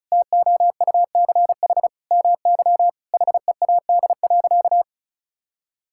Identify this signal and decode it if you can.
Morse code, heard as a keyed tone, about 34 wpm, 700 Hz. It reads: TOUCH MY HEAD.
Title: Sentences from Top 1600 Words 35wpm